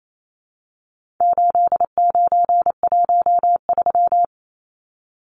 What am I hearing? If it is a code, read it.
Morse code, heard as a keyed tone, about 28 wpm, 700 Hz. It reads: 8913
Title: Numbers - 4-Digit 28wpm